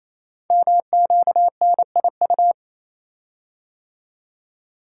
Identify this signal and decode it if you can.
Morse code, heard as a keyed tone, about 27 wpm, 700 Hz. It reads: MQNIU
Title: Code Groups - 5 Letters - 28wpm